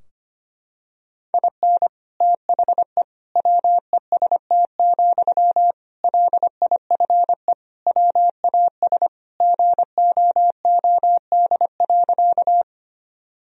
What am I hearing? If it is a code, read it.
Morse code, heard as a keyed tone, about 25 wpm, 700 Hz. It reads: IN THE WEST, LIFE WAS GOOD.
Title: Sentences from Top 1000 Words - Review Part 2 - 25wpm